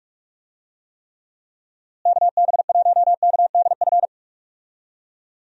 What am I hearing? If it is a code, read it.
Morse code, heard as a keyed tone, about 46 wpm, 700 Hz. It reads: KB1KDF
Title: USA Call Signs - 6 Characters - Repeated 3x - 45wpm